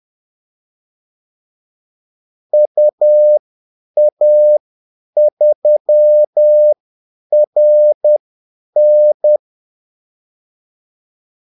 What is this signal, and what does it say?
Morse code, heard as a keyed tone, about 10 wpm, spaced as for 8 wpm, 600 Hz. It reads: UA3RN
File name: Callsigns-10-8wpm